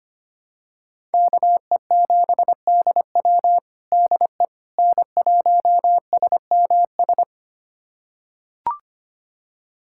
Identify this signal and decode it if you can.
Morse code, heard as a keyed tone, about 25 wpm, 700 Hz. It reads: KE7DW DE N1SMS
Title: USA Callsign DE Callsign - 5 Characters - 25wpm